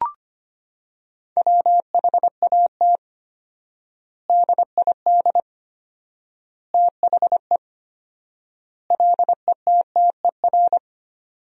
Morse code, heard as a keyed tone, about 25 wpm, 700 Hz. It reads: WHAT DID THE LETTER
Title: 4x Word-Spacing - Sentences from Top 500 Words - 25wpm